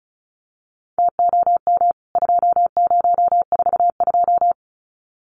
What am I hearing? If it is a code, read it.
Morse code, heard as a keyed tone, about 34 wpm, 700 Hz. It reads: TOM 2042
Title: CWT Exchange 35wpm